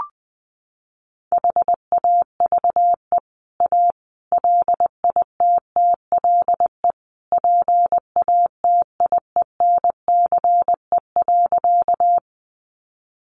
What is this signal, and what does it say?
Morse code, heard as a keyed tone, about 20 wpm, 700 Hz. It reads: HAVE A LITTLE PATIENCE.
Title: Sentences from Top 1300 Words 20wpm